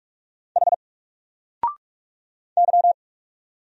Morse code, heard as a keyed tone, about 48 wpm, 700 Hz. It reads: H Y
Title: Single Letter-Number Alt 45wpm